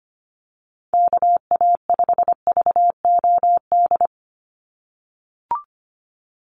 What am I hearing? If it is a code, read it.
Morse code, heard as a keyed tone, about 25 wpm, 690 Hz. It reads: KA5VOD
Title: USA Call Signs 25wpm